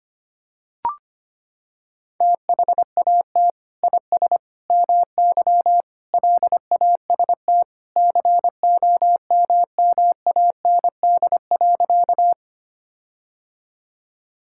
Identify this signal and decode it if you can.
Morse code, heard as a keyed tone, about 25 wpm, 700 Hz. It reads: THAT IS MY LAST COMMAND.
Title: Sentences from Top 1600 Words 25wpm